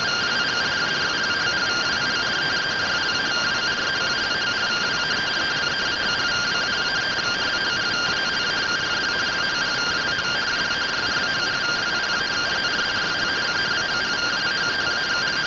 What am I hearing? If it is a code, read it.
Two-tone teleprinter signal, about 170 Hz shift.